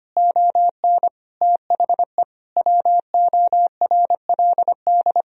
Morse code, heard as a keyed tone, about 25 wpm, 700 Hz. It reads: ON THE WORLD
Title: Sentences from Top 1000 Words - Review Part 4 - 25wpm